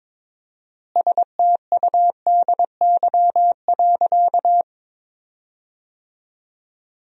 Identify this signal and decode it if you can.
Morse code, heard as a keyed tone, about 22 wpm, 700 Hz. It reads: STUDY.
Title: 4x Word-Spacing - Sentences from Top 300 Words - 22wpm